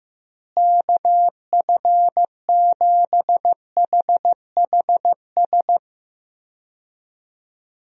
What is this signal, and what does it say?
Morse code, heard as a keyed tone, about 15 wpm, 700 Hz. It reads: KF7HHS
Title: USA Call Signs - 6 Characters - Repeated 2x - 15wpm